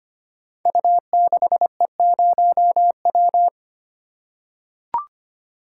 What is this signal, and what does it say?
Morse code, heard as a keyed tone, about 25 wpm, 700 Hz. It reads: U6E0W
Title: Code Groups - 5 Letter:Numbers - 25wpm